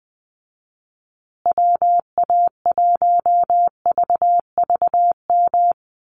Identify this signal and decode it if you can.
Morse code, heard as a keyed tone, about 20 wpm, 700 Hz. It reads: WA1VVM
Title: USA Call Signs Repeated 2x 20wpm